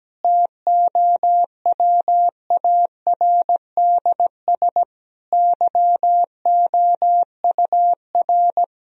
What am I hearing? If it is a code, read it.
Morse code, heard as a keyed tone, about 17 wpm, 700 Hz. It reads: TOWARDS YOUR
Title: Sentences from Top 1000 Words - Review Part 2 - 17wpm